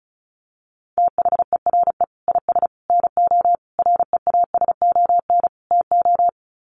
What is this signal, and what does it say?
Morse code, heard as a keyed tone, about 34 wpm, 700 Hz. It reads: THERE IS NO REASON TO
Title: Sets of 5 Words Encore 35wpm